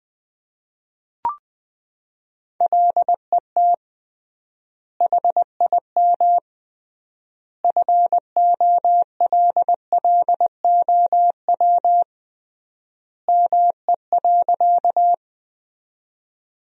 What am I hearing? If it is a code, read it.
Morse code, heard as a keyed tone, about 20 wpm, 700 Hz. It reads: LET HIM FOLLOW ME.
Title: 3x Word-Spacing - Sentences from Top 300 Words - 20wpm